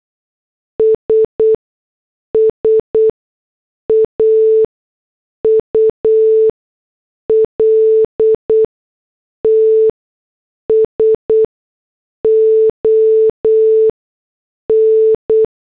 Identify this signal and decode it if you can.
Morse code, heard as a keyed tone, about 8 wpm, spaced as for 6 wpm, 440 Hz. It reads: SSAULTS ON